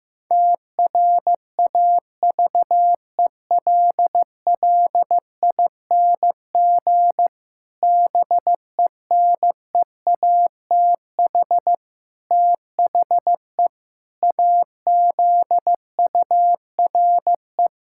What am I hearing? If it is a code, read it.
Morse code, heard as a keyed tone, about 15 wpm, 700 Hz. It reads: TRAVELLING BENEATH THE AZURE
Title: Pangram Challenge v2 - 15wpm